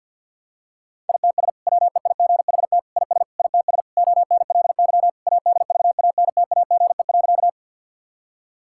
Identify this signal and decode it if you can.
Morse code, heard as a keyed tone, about 50 wpm, 700 Hz. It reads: ITS WEIGHT IS ITS ONLY ADVANTAGE.
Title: Sentences from Top 1000 Words - Review Part 6 - 50wpm